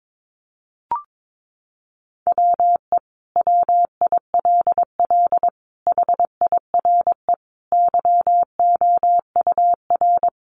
Morse code, heard as a keyed tone, about 22 wpm, 700 Hz. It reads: WE WILL HIRE YOUR
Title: Sentences from Top 1400 Words 22wpm